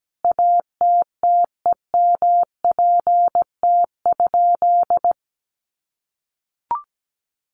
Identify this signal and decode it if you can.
Morse code, heard as a keyed tone, about 17 wpm, 700 Hz. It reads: ATTEMPT?
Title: Sentences from Top 1000 Words 17wpm